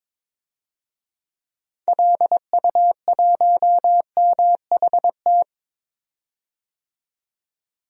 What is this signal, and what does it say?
Morse code, heard as a keyed tone, about 21 wpm, 700 Hz. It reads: LU1MHT
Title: DXCC-100 Call Signs - 6 Characters - 22wpm